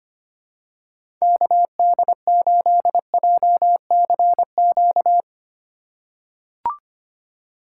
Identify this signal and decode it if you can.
Morse code, heard as a keyed tone, about 25 wpm, 700 Hz. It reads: KD8JCQ
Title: USA Call Signs - 6 Characters - 3x Character Spacing - 25wpm